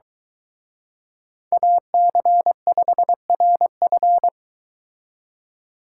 Morse code, heard as a keyed tone, about 23 wpm, 700 Hz. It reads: AC5RF